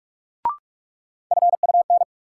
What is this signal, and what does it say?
Morse code, heard as a keyed tone, about 43 wpm, 700 Hz. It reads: FUN
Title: Sentences from Top 700 Words - New Word Drill - 45wpm